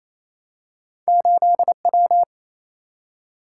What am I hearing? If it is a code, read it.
Morse code, heard as a keyed tone, about 28 wpm, 700 Hz. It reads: 8W
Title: Single-Letter Single-Number 28wpm